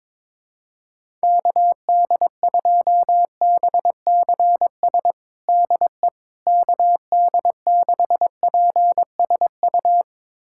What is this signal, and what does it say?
Morse code, heard as a keyed tone, about 22 wpm, 700 Hz. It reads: KD2BCS DE KD6PSU
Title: USA Callsign DE Callsign 22wpm